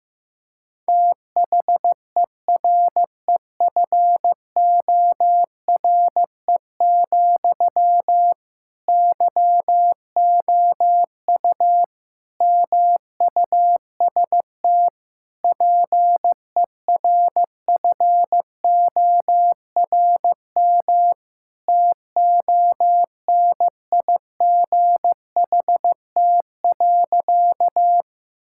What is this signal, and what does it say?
Morse code, heard as a keyed tone, about 15 wpm, 700 Hz. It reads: THEREFORE, YOU MUST PERFORM TONIGHT.
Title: Sentences from Top 1000 Words - Review Part 5 - 15wpm